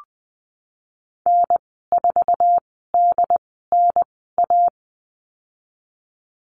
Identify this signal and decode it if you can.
Morse code, heard as a keyed tone, about 20 wpm, spaced as for 15 wpm, 700 Hz. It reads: N4DNA
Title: 128 Lesson 21g - Number 6 - Call Signs 20wpm